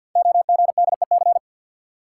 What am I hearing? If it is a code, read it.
Morse code, heard as a keyed tone, about 49 wpm, 700 Hz. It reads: OGDEC
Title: Code Groups - 5 Letters - 50wpm